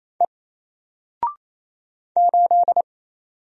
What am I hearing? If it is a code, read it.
Morse code, heard as a keyed tone, about 28 wpm, 700 Hz. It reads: E 8
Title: Single Letter-Number Alt 28wpm